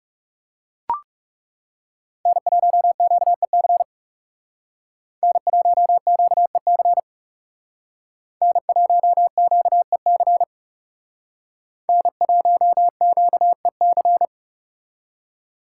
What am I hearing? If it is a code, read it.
Morse code, heard as a keyed tone, about 39 wpm, 700 Hz. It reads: N1QEC N1QEC N1QEC N1QEC